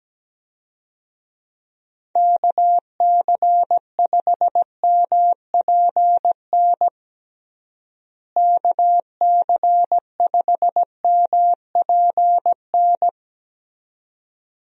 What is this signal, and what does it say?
Morse code, heard as a keyed tone, about 17 wpm, 700 Hz. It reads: KC5MPN KC5MPN
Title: USA Call Signs - 6 Characters - Repeated 2x - 17wpm